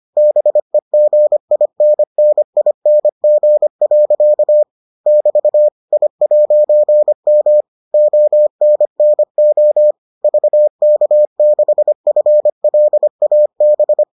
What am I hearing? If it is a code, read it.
Morse code, heard as a keyed tone, about 25 wpm, 600 Hz. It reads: BEGINNING. <BT> I'M ONNO VK6FLAB